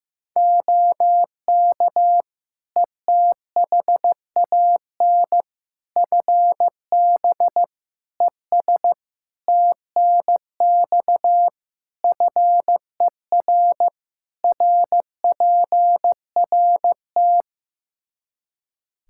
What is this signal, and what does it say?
Morse code, heard as a keyed tone, about 15 wpm, 700 Hz. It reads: OK ETHAN FB ES TNX FER RPRT
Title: Practice QSO - Exchange 2 - Info - 15wpm